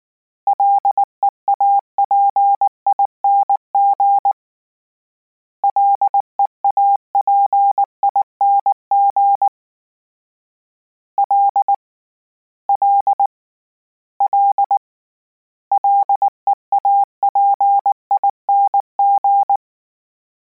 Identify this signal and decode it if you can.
Morse code, heard as a keyed tone, about 19 wpm, 800 Hz. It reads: LEAPING LEAPING L L L LEAPING